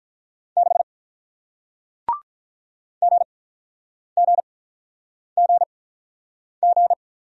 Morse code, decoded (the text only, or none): 6 GGGG